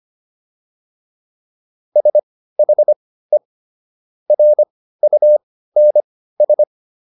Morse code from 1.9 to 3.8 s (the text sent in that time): SHE